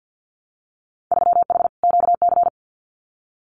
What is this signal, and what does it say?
Morse code, heard as a keyed tone, about 50 wpm, 700 Hz. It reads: 3H QC
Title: Winter Field Day Exchanges 50wpm